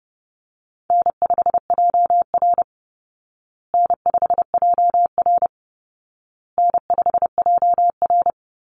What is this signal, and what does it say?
Morse code, heard as a keyed tone, about 30 wpm, 700 Hz. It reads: N5JR N5JR N5JR
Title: Easy Call Signs Repeated 3x - 30wpm